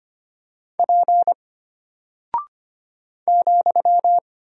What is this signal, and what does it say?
Morse code, heard as a keyed tone, about 25 wpm, 700 Hz. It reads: P ,
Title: Single Character (Letter:Number:Punctuation:Prosign) Alt 25wpm